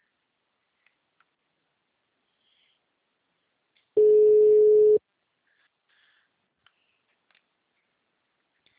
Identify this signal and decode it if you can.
Ringback tone (Europe, 425 Hz).
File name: out- -2002- - - .281